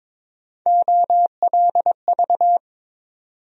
Morse code, heard as a keyed tone, about 22 wpm, 700 Hz. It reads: OLV
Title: USA Call Sign Postfixes - 3 Letters 22wpm